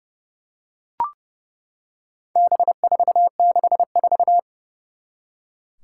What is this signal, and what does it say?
Morse code, heard as a keyed tone, about 30 wpm, 700 Hz. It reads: B464
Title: Four Challenging Letter-Number Combinations 30wpm